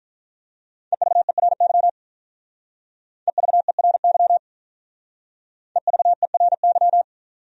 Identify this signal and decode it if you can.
Morse code, heard as a keyed tone, about 42 wpm, 700 Hz. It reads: EVERY EVERY EVERY
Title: Speed-Racing Top 500 Words 35wpm